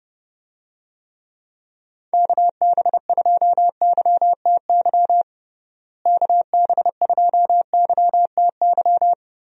Morse code, heard as a keyed tone, about 30 wpm, 700 Hz. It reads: KB2YTY KB2YTY
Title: USA Call Signs - 6 Characters - Repeated 3x - 30wpm